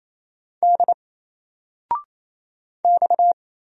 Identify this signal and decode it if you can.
Morse code, heard as a keyed tone, about 29 wpm, 700 Hz. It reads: D X
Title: Single Letter-Number 28wpm